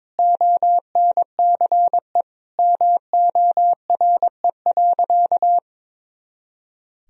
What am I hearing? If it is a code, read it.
Morse code, heard as a keyed tone, about 22 wpm, 700 Hz. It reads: ONCE MORE.
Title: Sentences from Top 500 Words and Abbreviations 22wpm